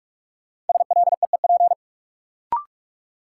Morse code, heard as a keyed tone, about 42 wpm, 700 Hz. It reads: SLEEP
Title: Sentences from Top 500 Words - New Word Drill - 45wpm